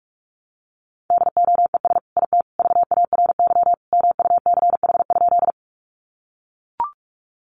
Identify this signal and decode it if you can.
Morse code, heard as a keyed tone, about 44 wpm, 700 Hz. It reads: DOES IT VARY MUCH?
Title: Sentences from Top 1200 Words 45wpm